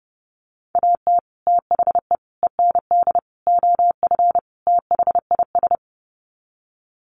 Morse code, heard as a keyed tone, about 30 wpm, 700 Hz. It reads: AT THE END OF THIS
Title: Sets of 5 Words 30wpm